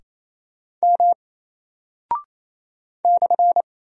Morse code, heard as a keyed tone, about 28 wpm, 700 Hz. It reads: M /
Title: Single Character (Letter:Number:Punctuation:Prosign) Alt 28wpm